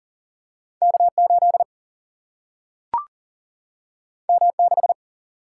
Morse code, heard as a keyed tone, about 40 wpm, 700 Hz. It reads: K 8 M 6
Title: Single-Letter Single-Number 40wpm